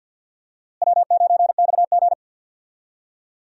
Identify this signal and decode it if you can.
Morse code, heard as a keyed tone, about 46 wpm, 700 Hz. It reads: W9XG
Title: USA Call Signs - 4 Characters - Repeated 2x - 50wpm